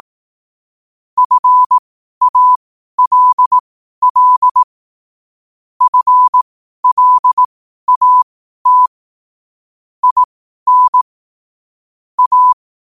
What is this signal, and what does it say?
Morse code, heard as a keyed tone, about 18 wpm, spaced as for 12 wpm, 1000 Hz. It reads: FALL FLAT IN A
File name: qotdfiles / 12wpm